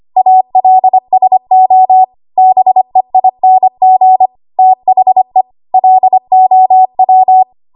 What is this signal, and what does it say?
Morse code, heard as a keyed tone, about 25 wpm, 750 Hz. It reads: ALSO BEING THE LOW
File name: Archive / 25 WPM / 25WPM